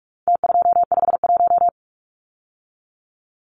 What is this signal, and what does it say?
Morse code, heard as a keyed tone, about 44 wpm, 700 Hz. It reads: T251
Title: Numbers - Contest Style - 4-Digit 45wpm